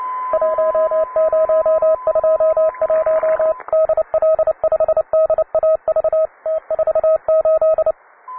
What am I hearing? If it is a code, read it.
Morse code, heard as a keyed tone, about 29 wpm, 640 Hz, with radio noise. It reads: 1021 DL5DAV T48